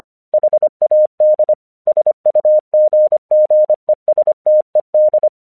Morse code, heard as a keyed tone, about 25 wpm, 610 Hz. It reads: HAD SUGGESTED
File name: 1 English 25wpm / AQC17